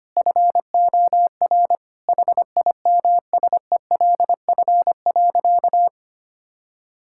Morse code, heard as a keyed tone, about 25 wpm, 700 Hz. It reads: FOR HIMSELF.
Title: Sentences from Top 1600 Words 25wpm